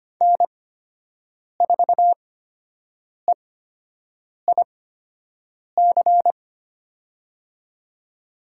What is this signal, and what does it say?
Morse code, heard as a keyed tone, about 25 wpm, spaced as for 7 wpm, 700 Hz. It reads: N4EIC
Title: USA Call Signs - 5 Characters - 8x Character Spacing - 25wpm